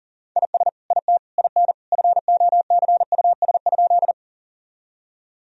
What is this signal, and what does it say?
Morse code, heard as a keyed tone, about 37 wpm, 700 Hz. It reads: IS IT IN FOCUS?
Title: Sentences from Top 1000 Words - Review Part 4 - 40wpm